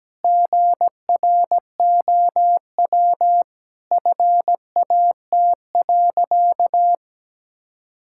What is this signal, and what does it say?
Morse code, heard as a keyed tone, about 17 wpm, 700 Hz. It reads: GROW FAT.
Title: Sentences from Top 1000 Words - Review Part 5 - 17wpm